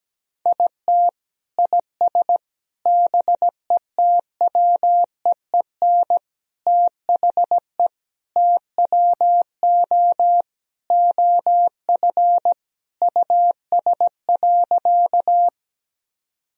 Morse code, heard as a keyed tone, about 17 wpm, 700 Hz. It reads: IT IS BETWEEN THE TWO OF US.
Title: Sentences from Top 1000 Words - Review Part 1 - 17wpm